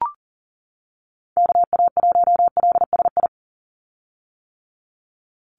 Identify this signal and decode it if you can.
Morse code, heard as a keyed tone, about 38 wpm, 690 Hz. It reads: KA1LSI
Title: USA Call Signs 40wpm